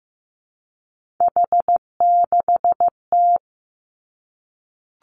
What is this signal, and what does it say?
Morse code, heard as a keyed tone, about 15 wpm, 700 Hz. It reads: H6T
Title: Three Challenging Letter-Number Combinations 15wpm